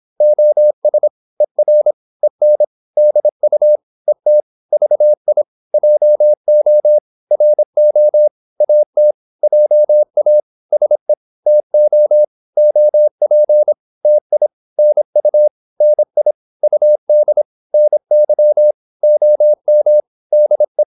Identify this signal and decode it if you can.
Morse code, heard as a keyed tone, about 26 wpm, 600 Hz. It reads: OS ER EN DU ET VI JO RO AT JA SE TO OP TI NU NI UD NY OM DE
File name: Korte ord DK 26wpm